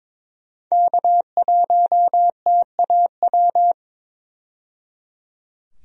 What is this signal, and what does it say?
Morse code, heard as a keyed tone, about 22 wpm, 700 Hz. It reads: K1TAW
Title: USA Call Signs - 5 Characters - 22wpm